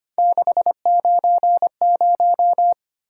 Morse code, heard as a keyed tone, about 25 wpm, 700 Hz. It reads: 690
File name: Three Letter-Number Combinations - 25wpm